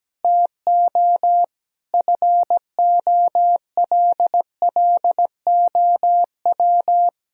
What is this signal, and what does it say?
Morse code, heard as a keyed tone, about 17 wpm, 700 Hz. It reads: TO FOLLOW
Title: Sentences from Top 1000 Words - Review Part 3 - 17wpm